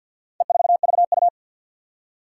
Morse code, heard as a keyed tone, about 51 wpm, 700 Hz. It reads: E4VU
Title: Four Challenging Letter-Number Combinations 50wpm